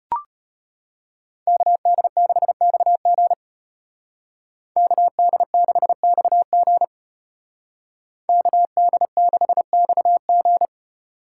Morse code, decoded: KD6XG KD6XG KD6XG

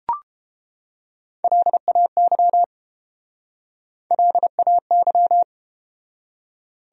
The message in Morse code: LAY LAY

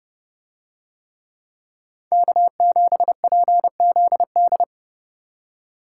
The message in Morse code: K7PZD

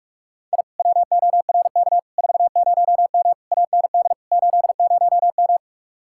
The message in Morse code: I WORK 40M AND 80M

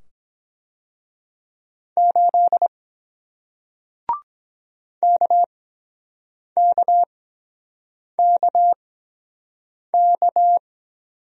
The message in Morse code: 8 KKKK

